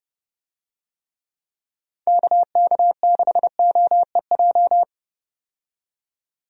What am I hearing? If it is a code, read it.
Morse code, heard as a keyed tone, about 30 wpm, 700 Hz. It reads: KK6OEJ
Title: USA Call Signs - 6 Characters - 30wpm